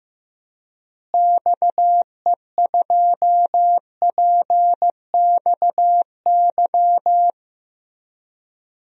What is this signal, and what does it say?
Morse code, heard as a keyed tone, about 16 wpm, 700 Hz. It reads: XE2PXY
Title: DXCC-100 Call Signs - 6 Characters - Repeated 3x - 15wpm